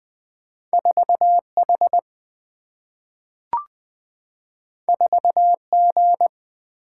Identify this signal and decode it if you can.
Morse code, heard as a keyed tone, about 20 wpm, 700 Hz. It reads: 4H 4G
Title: Single-Letter Single-Number 20wpm